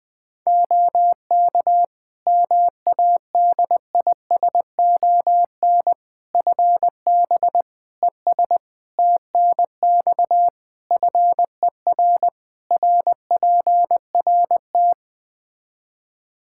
Morse code, heard as a keyed tone, about 20 wpm, 700 Hz. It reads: OK MADISON FB ES TNX FER RPRT
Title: Practice QSO - Exchange 2 - Info - 20wpm